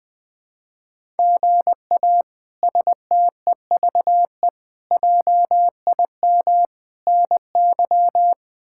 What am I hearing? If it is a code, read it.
Morse code, heard as a keyed tone, about 20 wpm, 700 Hz. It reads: GA STEVE JIM NY
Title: K1USN SST QSO Practice - 20wpm